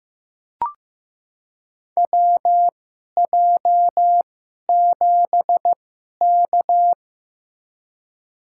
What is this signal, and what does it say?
Morse code, heard as a keyed tone, about 15 wpm, spaced as for 11 wpm, 700 Hz. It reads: WJ7K
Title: USA Call Signs - 4 Characters - 2x Character Spacing - 15wpm